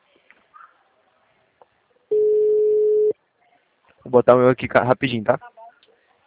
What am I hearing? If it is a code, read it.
Ringback tone (Europe, 425 Hz), with speech.